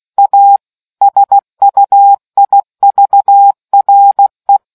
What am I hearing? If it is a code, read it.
Morse code, heard as a keyed tone, about 16 wpm, 800 Hz. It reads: A SUIVRE